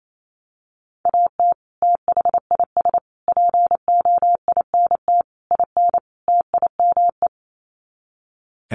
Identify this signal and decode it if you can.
Morse code, heard as a keyed tone, about 28 wpm, 700 Hz. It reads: AT THIS POINT IN TIME
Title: Sets of 5 Words Encore 28wpm